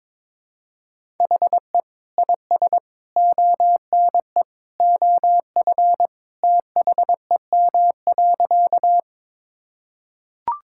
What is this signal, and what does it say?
Morse code, heard as a keyed tone, about 22 wpm, 700 Hz. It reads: HE IS ONE OF THEM.
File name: practice / 22wpm / Sentences with 3-Word Phrases - 22wpm